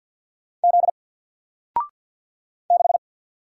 Morse code, heard as a keyed tone, about 50 wpm, 700 Hz. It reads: Z 6
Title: Single Character (Letter:Number:Punctuation:Prosign) Alt 50wpm